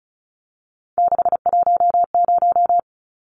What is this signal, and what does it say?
Morse code, heard as a keyed tone, about 34 wpm, 700 Hz. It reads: 610
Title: Numbers - 3-Digit 35wpm